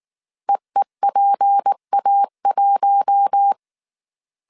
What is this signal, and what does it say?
Morse code, heard as a keyed tone, about 19 wpm, 800 Hz. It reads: EEPA1
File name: CW / DET / 1 6 1500C S A 18wpm 2255